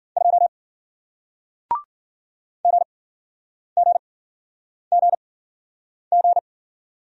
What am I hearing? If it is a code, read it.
Morse code, heard as a keyed tone, about 50 wpm, spaced as for 9 wpm, 700 Hz. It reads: 2 GGGG
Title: Speed-Racing Single Letter-Number 40wpm